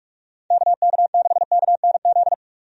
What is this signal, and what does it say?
Morse code, heard as a keyed tone, about 42 wpm, 700 Hz. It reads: KK6KNZ
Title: USA Call Signs - 6 Characters - 45wpm